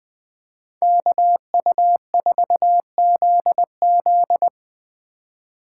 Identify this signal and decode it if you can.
Morse code, heard as a keyed tone, about 20 wpm, 700 Hz. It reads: KU4ZZ
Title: Speed Racing - USA Call Signs 17wpm